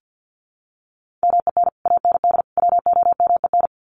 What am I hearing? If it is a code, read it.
Morse code, heard as a keyed tone, about 48 wpm, 700 Hz. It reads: MEN AND WOMEN
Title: Sets of 3 Words 50wpm